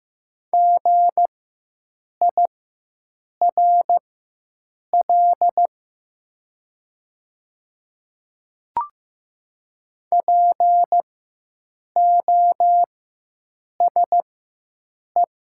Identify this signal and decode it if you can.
Morse code, heard as a keyed tone, about 15 wpm, spaced as for 7 wpm, 700 Hz. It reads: GIRL POSE
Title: 4x Character Spacing - Top 500 Words - 15wpm